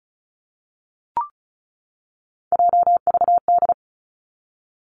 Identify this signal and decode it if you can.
Morse code, heard as a keyed tone, about 36 wpm, 700 Hz. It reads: JVD
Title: Three Challenging Letter-Number Combinations 35wpm